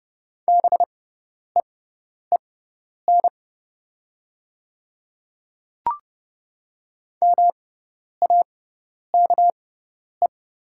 Morse code, heard as a keyed tone, about 30 wpm, spaced as for 10 wpm, 700 Hz. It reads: BEEN MAKE
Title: 6x Character Spacing - Top 100 Words - 30wpm